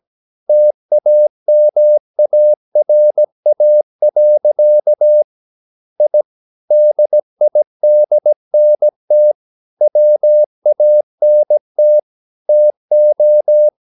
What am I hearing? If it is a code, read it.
Morse code, heard as a keyed tone, about 17 wpm, 600 Hz. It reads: TAMARA. I DIDNT WANT TO
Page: Romanen Never af Ken Follet på engelsk.